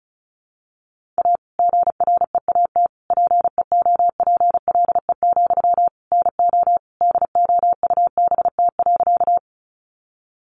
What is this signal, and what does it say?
Morse code, heard as a keyed tone, about 34 wpm, 700 Hz. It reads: A GREAT PEOPLE, NO DOUBT.
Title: Sentences from Top 1500 Words 35wpm